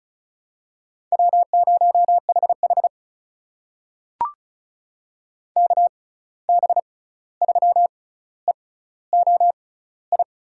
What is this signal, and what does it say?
Morse code, heard as a keyed tone, about 36 wpm, spaced as for 16 wpm, 700 Hz. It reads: W0HH KB3EOI
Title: USA Call Signs - 6x Character Spacing - 35wpm